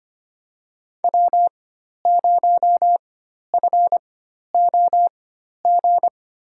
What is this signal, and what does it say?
Morse code, heard as a keyed tone, about 25 wpm, spaced as for 12 wpm, 700 Hz. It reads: W0FOG
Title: USA Call Signs - 5 Characters - 4x Character Spacing - 25wpm